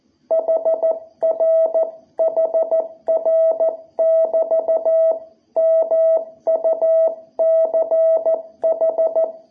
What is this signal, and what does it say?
Morse code, heard as a keyed tone, about 14 wpm, 660 Hz. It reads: HR HR <BT> MUCH
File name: jh8cbh / 2021cw